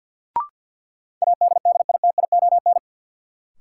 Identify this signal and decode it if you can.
Morse code, heard as a keyed tone, about 52 wpm, 700 Hz. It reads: ADDITION
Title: Sets of 3 Words - New Word Drill - 50wpm